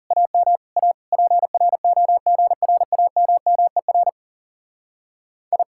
Morse code, heard as a keyed tone, about 36 wpm, 700 Hz. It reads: AM A PROGRAMMER I
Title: Speed-Racing Common QSO Phrases - 30wpm